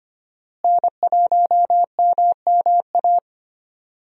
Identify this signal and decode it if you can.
Morse code, heard as a keyed tone, about 25 wpm, 700 Hz. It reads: N1MMA
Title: USA Call Signs - 5 Characters - Repeated 2x - 25wpm